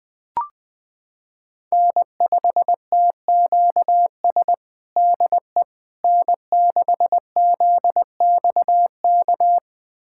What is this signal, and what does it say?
Morse code, heard as a keyed tone, about 20 wpm, 700 Hz. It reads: N5TQS DE N6ZXK
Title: USA Callsign DE Callsign - 5 Characters - 20wpm